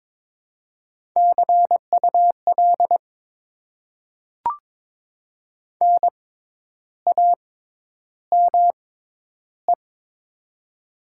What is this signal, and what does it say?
Morse code, heard as a keyed tone, about 22 wpm, spaced as for 10 wpm, 700 Hz. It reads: CUL NAME
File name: Top 90 QSO Elements - 6x Character Spacing - 22wpm